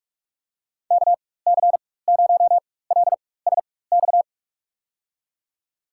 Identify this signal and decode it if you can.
Morse code, heard as a keyed tone, about 46 wpm, spaced as for 21 wpm, 700 Hz. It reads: KC0LSX
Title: USA Call Signs - 6 Characters - 4x Character Spacing - 45wpm